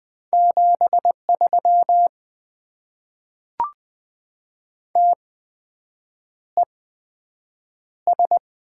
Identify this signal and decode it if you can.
Morse code, heard as a keyed tone, about 20 wpm, spaced as for 7 wpm, 700 Hz. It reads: 73 TES
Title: Top 90 QSO Elements - 8x Character Spacing - 20wpm